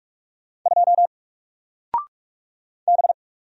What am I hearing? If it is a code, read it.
Morse code, heard as a keyed tone, about 46 wpm, 700 Hz. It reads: 2 B
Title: Single Character (Letter:Number:Punctuation:Prosign) 45wpm